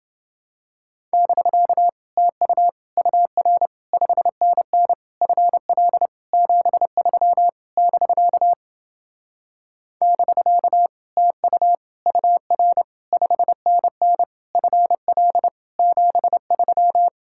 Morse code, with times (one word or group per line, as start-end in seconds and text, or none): 1.1-1.9 <BK>
2.2-2.7 TU
3.0-3.7 UR
3.9-4.9 5NN
5.2-6.1 FL
6.3-7.5 73
7.8-8.5 <BK>
10.0-10.9 <BK>
11.2-11.8 TU
12.1-12.8 UR
13.1-14.2 5NN
14.6-15.5 FL
15.8-17.1 73